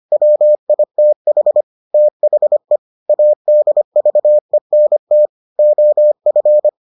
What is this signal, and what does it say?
Morse code, heard as a keyed tone, about 25 wpm, 600 Hz. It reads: WITH THE ADVENT OF